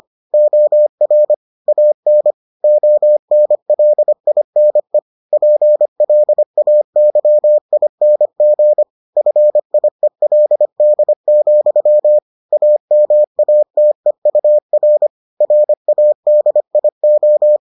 Morse code, heard as a keyed tone, about 25 wpm, 600 Hz. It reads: OR AN ONLINE PLAYING FIELD, AMATEUR RADIO